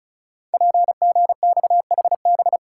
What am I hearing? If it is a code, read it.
Morse code, heard as a keyed tone, about 36 wpm, 700 Hz. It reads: PGXHB
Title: Code Groups - 5 Letters - 35wpm